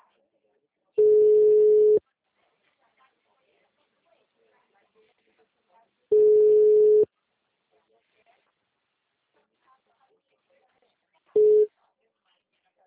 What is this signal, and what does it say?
Ringback tone (Europe, 425 Hz), with speech.